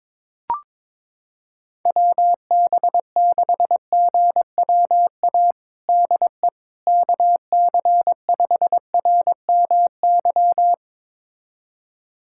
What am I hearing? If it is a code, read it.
Morse code, heard as a keyed tone, about 22 wpm, 700 Hz. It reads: WB6GWA DE KC5RMY
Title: USA Callsign DE Callsign 22wpm